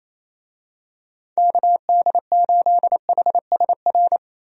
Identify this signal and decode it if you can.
Morse code, heard as a keyed tone, about 27 wpm, 700 Hz. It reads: KD8HSR
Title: USA Call Signs - 6 Characters - Repeated 2x - 28wpm